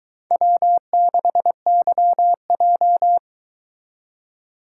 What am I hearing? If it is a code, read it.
Morse code, heard as a keyed tone, about 22 wpm, 700 Hz. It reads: W6YJ